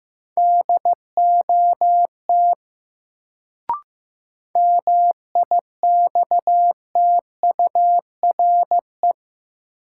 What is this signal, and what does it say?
Morse code, heard as a keyed tone, about 15 wpm, 700 Hz. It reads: DOT MIXTURE
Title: Sentences from Top 1500 Words - New Word Drill - 15wpm